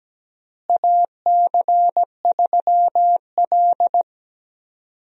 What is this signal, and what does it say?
Morse code, heard as a keyed tone, about 17 wpm, 700 Hz. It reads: AC3L
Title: USA Call Signs - 4 Characters - Repeated 3x - 17wpm